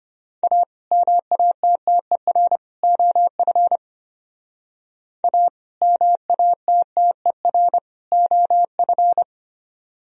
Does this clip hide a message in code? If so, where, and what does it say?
Yes: Morse code from 0.4 to 9.2 s, heard as a keyed tone, about 32 wpm, 700 Hz. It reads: A MATTER OF A MATTER OF